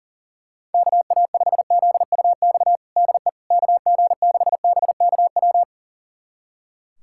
Morse code, heard as a keyed tone, about 37 wpm, 700 Hz. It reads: KA5ZUX DE KG6BKW
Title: USA Callsign DE Callsign 40wpm